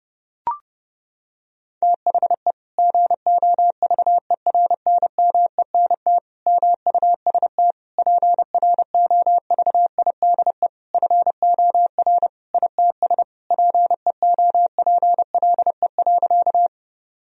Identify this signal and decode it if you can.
Morse code, heard as a keyed tone, about 32 wpm, 700 Hz. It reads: THE GOVERNMENT MUST PROVIDE FOR ITS PEOPLE.
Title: Sentences from Top 1000 Words - Review Part 3 - 30wpm